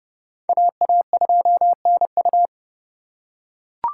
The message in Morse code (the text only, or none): AA2NU